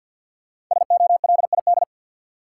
Morse code, heard as a keyed tone, about 47 wpm, 700 Hz. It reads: SOLID
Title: Sentences from Top 1500 Words - New Word Drill Alt - 50wpm